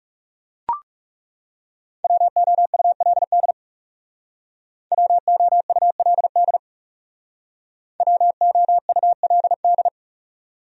Morse code, heard as a keyed tone, about 36 wpm, 700 Hz. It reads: WOULD WOULD WOULD
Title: Speed-Racing Top 100 Words 30wpm